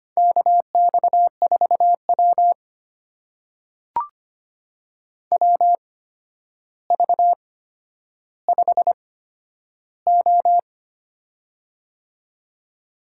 Morse code, heard as a keyed tone, about 25 wpm, spaced as for 11 wpm, 700 Hz. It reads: KX4W WV5O
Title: USA Call Signs - 4 Characters - 8x Character Spacing - 25wpm